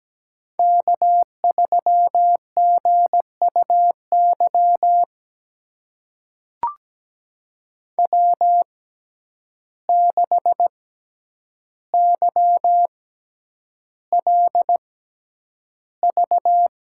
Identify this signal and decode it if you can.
Morse code, heard as a keyed tone, about 17 wpm, spaced as for 9 wpm, 700 Hz. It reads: K3GUY W6YLV
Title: USA Call Signs - 5 Characters - 6x Character Spacing - 17wpm